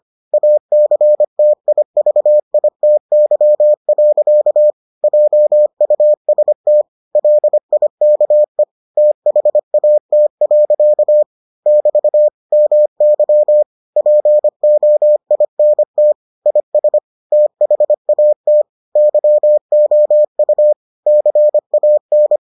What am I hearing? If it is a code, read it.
Morse code, heard as a keyed tone, about 25 wpm, 600 Hz. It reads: ACTIVITY. JUST LIKE THAT. <BT> MY POINT IS THAT YOU CAN